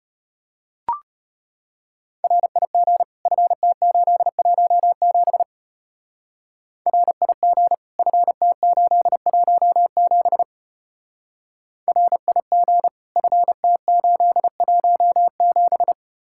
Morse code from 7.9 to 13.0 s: FT817 RIG

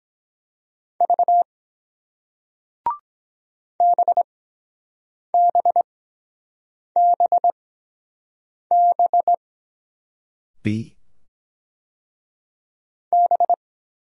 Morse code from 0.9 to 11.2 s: V BBBB

